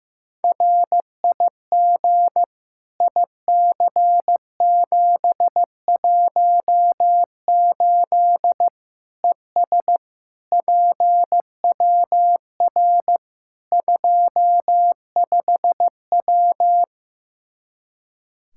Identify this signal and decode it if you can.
Morse code, heard as a keyed tone, about 15 wpm, 700 Hz. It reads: RIG IC718 ES PWR 25W
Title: Practice QSO - Exchange 2 - Info - 15wpm